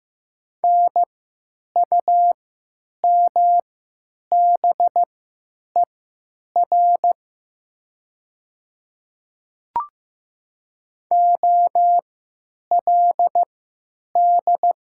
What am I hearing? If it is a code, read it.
Morse code, heard as a keyed tone, about 15 wpm, spaced as for 9 wpm, 700 Hz. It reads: NUMBER OLD
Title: 3x Character Spacing - Top 300 Words - 15wpm